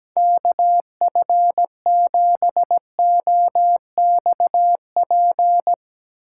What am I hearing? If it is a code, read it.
Morse code, heard as a keyed tone, about 17 wpm, 700 Hz. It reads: KF7OXP
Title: USA Call Signs - 6 Characters - 4x Character Spacing - 17wpm